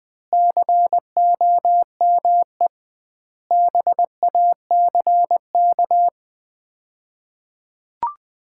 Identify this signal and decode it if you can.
Morse code, heard as a keyed tone, about 20 wpm, 700 Hz. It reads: COME BACK
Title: 2x Word-Spacing - Sets of 5 Words - 20wpm